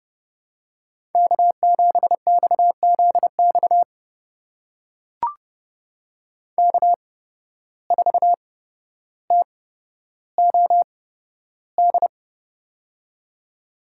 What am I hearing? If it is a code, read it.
Morse code, heard as a keyed tone, about 30 wpm, spaced as for 13 wpm, 700 Hz. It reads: K7XZX K4TOD
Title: USA Call Signs - 5 Characters - 8x Character Spacing - 30wpm